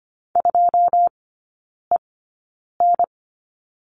Morse code, heard as a keyed tone, about 25 wpm, spaced as for 9 wpm, 690 Hz. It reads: 2EN
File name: Lesson 15e - Number 5 - Three Characters 25wpm